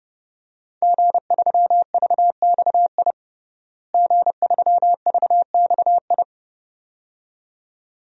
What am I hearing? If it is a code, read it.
Morse code, heard as a keyed tone, about 32 wpm, 700 Hz. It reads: G3VXI G3VXI
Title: DXCC-100 Call Signs - 5 Characters - Repeated 2x - 30wpm